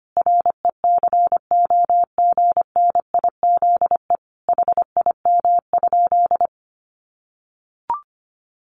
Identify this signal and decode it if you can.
Morse code, heard as a keyed tone, about 25 wpm, 700 Hz. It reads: RECOGNIZE HIM?
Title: Sentences from Top 1400 Words 25wpm